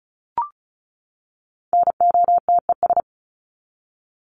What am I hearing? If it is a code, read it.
Morse code, heard as a keyed tone, about 34 wpm, 700 Hz. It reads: NOTES
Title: ES:S Suffix - Top 100 Words with Base Words 35wpm